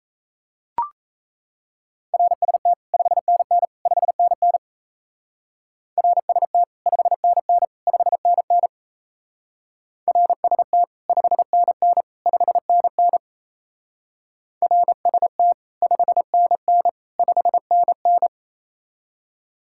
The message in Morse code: RST 5NN 5NN RST 5NN 5NN RST 5NN 5NN RST 5NN 5NN

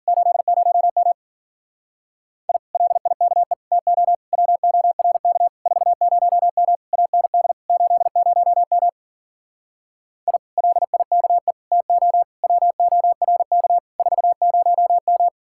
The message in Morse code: 80M I LIKE TO WORK 40M AND 80M I LIKE TO WORK 40M

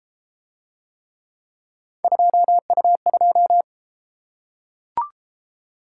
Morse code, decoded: 2U2

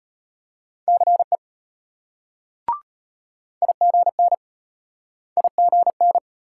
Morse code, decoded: CE IGN IGN

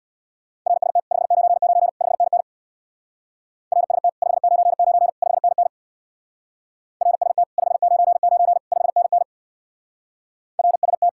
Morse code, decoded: RST 599 5NN RST 599 5NN RST 599 5NN RST

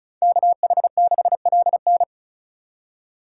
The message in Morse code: KH6LN

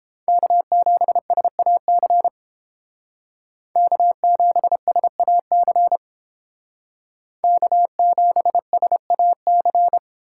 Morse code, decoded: K7SAC K7SAC K7SAC

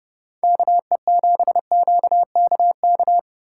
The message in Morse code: KE7QKK

